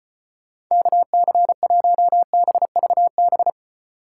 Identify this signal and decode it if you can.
Morse code, heard as a keyed tone, about 33 wpm, 700 Hz. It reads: KC1BVB